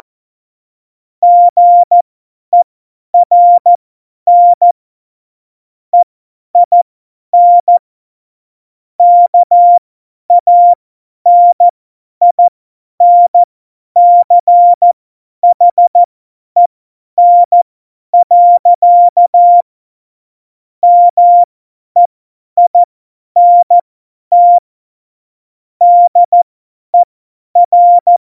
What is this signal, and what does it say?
Morse code, heard as a keyed tone, about 14 wpm, spaced as for 10 wpm, 700 Hz. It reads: GERN EIN KANINCHEN. MEINT DER
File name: Morsekurs / Witze / Witz 22. Januar 2019, 50 F70